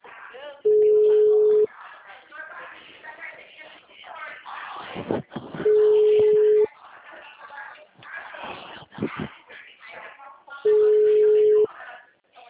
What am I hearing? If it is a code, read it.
Ringback tone (Europe, 425 Hz), with speech.